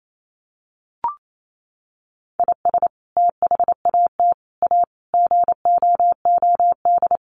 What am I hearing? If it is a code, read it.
Morse code, heard as a keyed tone, about 28 wpm, 700 Hz. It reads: IS THAT A GOOD
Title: Sentences from Top 1500 Words 28wpm